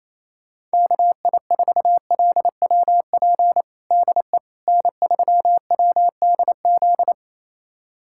Morse code, heard as a keyed tone, about 27 wpm, 700 Hz. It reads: KI4LWP DE N3WDZ
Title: USA Callsign DE Callsign 28wpm